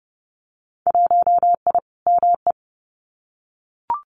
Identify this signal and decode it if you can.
Morse code, heard as a keyed tone, about 30 wpm, 700 Hz. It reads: 1I ME
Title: Winter Field Day Exchanges 30wpm